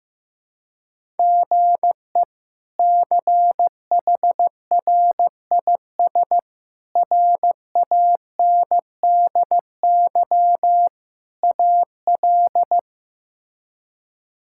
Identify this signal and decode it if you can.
Morse code, heard as a keyed tone, about 15 wpm, 700 Hz. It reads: GE CHRIS RANDY AL
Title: K1USN SST QSO Practice - 15wpm